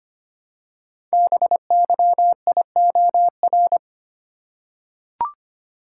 Morse code, heard as a keyed tone, about 25 wpm, 700 Hz. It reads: BYIOR
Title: Code Groups - 5 Letters - 25wpm